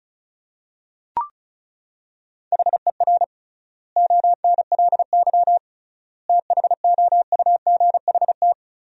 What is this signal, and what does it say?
Morse code, heard as a keyed tone, about 36 wpm, 700 Hz. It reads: HER ONLY THOUGHT
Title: 3x Word-Spacing - Sentences from Top 400 Words - 35wpm